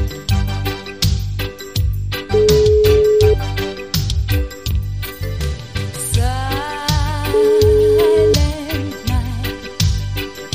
Ringback tone (Europe, 425 Hz).